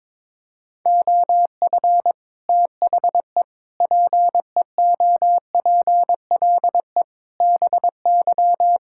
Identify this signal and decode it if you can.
Morse code, heard as a keyed tone, about 22 wpm, 700 Hz. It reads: OF THE PEOPLE BY
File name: practice / 22wpm / Sets of 8 Words - 22wpm